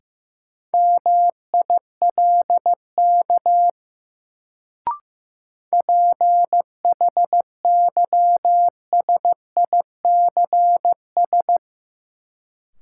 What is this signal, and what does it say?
Morse code, heard as a keyed tone, about 15 wpm, 700 Hz. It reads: MILK PHYSICS
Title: Sentences from Top 1500 Words - New Word Drill Alt - 15wpm